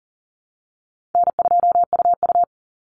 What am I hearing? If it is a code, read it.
Morse code, heard as a keyed tone, about 40 wpm, 700 Hz. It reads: N2UU
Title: Easy Call Signs Repeated 2x - 40wpm